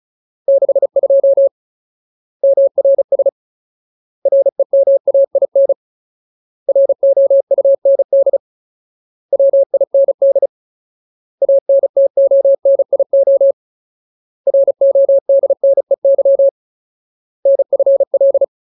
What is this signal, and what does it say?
Morse code, heard as a keyed tone, about 37 wpm, 550 Hz. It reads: 62 MRS REMAIN ROUND WIND ANTONIO RODNEY NFL
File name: Ultimate Word List / 35WPM Ultimate